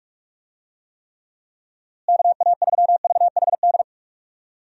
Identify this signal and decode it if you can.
Morse code, heard as a keyed tone, about 46 wpm, 700 Hz. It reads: KA3VHD
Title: USA Call Signs - 6 Characters - Repeated 2x - 45wpm